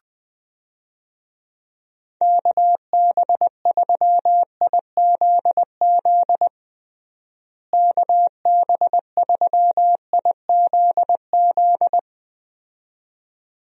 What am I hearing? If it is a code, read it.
Morse code, heard as a keyed tone, about 20 wpm, 700 Hz. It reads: KB3IZZ KB3IZZ
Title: USA Call Signs - 6 Characters - Repeated 2x - 20wpm